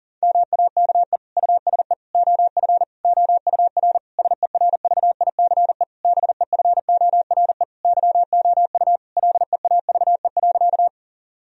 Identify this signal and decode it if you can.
Morse code, heard as a keyed tone, about 40 wpm, 700 Hz. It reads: MAKE USE OF OUR SERVICE BEFORE YOU LEAVE.
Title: Sentences from Top 1000 Words - Review Part 4 - 40wpm